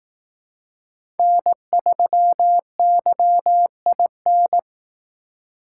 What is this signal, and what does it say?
Morse code, heard as a keyed tone, about 18 wpm, 700 Hz. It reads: N3YIN
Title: Speed Racing - USA 1x3 Call Signs 15wpm